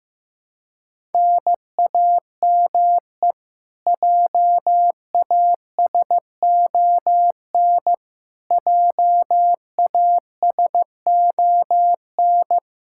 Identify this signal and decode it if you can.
Morse code, heard as a keyed tone, about 15 wpm, 700 Hz. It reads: NAME JASON JASON
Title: K1USN SST QSO Practice 15wpm